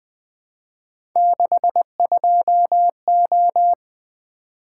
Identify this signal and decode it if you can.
Morse code, heard as a keyed tone, about 20 wpm, 700 Hz. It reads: 62O
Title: Three Letter-Number Combinations - 20wpm